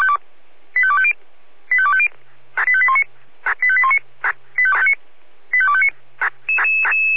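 Five-tone selective call (ZVEI), heard as quick runs of five tones, tones of about 70 ms. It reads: 85250 85250 85819 85819 85259
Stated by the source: ZVEI five-tone selective call received on SDR.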